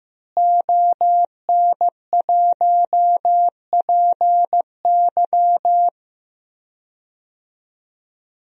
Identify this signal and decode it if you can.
Morse code, heard as a keyed tone, about 15 wpm, 700 Hz. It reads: ON1PY
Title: DXCC-100 Call Signs - 5 Characters - Repeated 2x - 15wpm